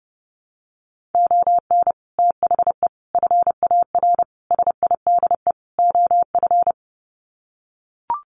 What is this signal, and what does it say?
Morse code, heard as a keyed tone, about 30 wpm, 700 Hz. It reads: ON THE FAR SIDE OF
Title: Sets of 5 Words 30wpm